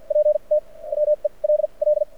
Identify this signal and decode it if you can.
Morse code, heard as a keyed tone, about 54 wpm, spaced as for 37 wpm, 600 Hz. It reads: PT WELL
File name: archived / CRSnet